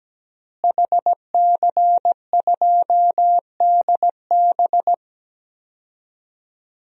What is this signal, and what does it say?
Morse code, heard as a keyed tone, about 17 wpm, 700 Hz. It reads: HC2DB
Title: DXCC-100 Call Signs - 5 Characters - Repeated 2x - 17wpm